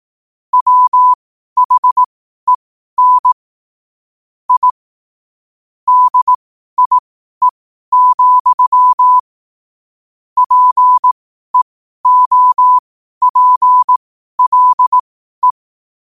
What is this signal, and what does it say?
Morse code, heard as a keyed tone, about 18 wpm, spaced as for 12 wpm, 1000 Hz. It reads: WHEN I DIE, PEOPLE
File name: qotdfiles / 12wpm